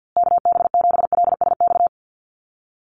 Morse code, heard as a keyed tone, about 50 wpm, 700 Hz. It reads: KB7LSX
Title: USA Call Signs Repeated 2x 50wpm